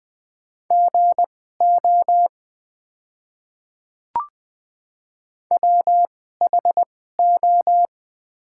Morse code, heard as a keyed tone, about 20 wpm, spaced as for 14 wpm, 700 Hz. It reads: GO WHO
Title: 2x Character Spacing - Top 200 Words - 20wpm